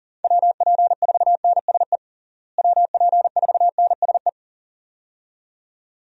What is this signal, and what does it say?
Morse code, heard as a keyed tone, about 40 wpm, 700 Hz. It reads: WP4NSE WP4NSE
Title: USA Call Signs - 6 Characters - Repeated 2x - 40wpm